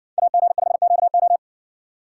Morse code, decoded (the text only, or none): AG5QO